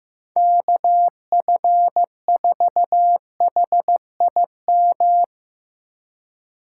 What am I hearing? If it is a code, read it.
Morse code, heard as a keyed tone, about 17 wpm, 700 Hz. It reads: KF4HIM
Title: Speed Racing - USA Call Signs 15wpm